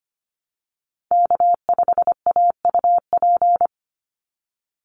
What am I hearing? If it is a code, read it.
Morse code, heard as a keyed tone, about 25 wpm, 700 Hz. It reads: K5AUP
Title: Call Signs Repeated 3x - 25wpm